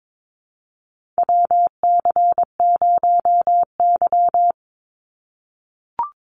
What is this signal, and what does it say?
Morse code, heard as a keyed tone, about 22 wpm, 700 Hz. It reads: WC0Y
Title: NA SOTA Chaser Callsigns 22wpm